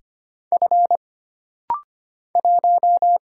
Morse code, heard as a keyed tone, about 25 wpm, 700 Hz. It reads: F 1
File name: Single Character (Letter:Number:Punctuation:Prosign) - Rapid-Fire - 25wpm